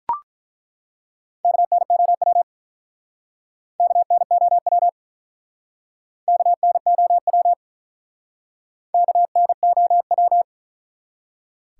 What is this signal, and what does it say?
Morse code, heard as a keyed tone, about 42 wpm, 700 Hz. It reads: KNOW KNOW KNOW KNOW
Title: Speed-Racing Top 180 QSO Elements 35wpm